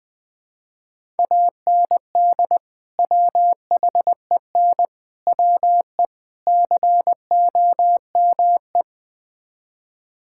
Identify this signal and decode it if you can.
Morse code, heard as a keyed tone, about 21 wpm, 700 Hz. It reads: AND WHEN WE COME
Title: Build A Phrase - 5-Word Phrases - 20wpm